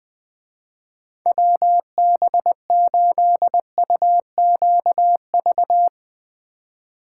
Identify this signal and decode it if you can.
Morse code, heard as a keyed tone, about 22 wpm, 700 Hz. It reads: WB8UQV
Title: Speed Racing - USA 2x3 Call Signs 20wpm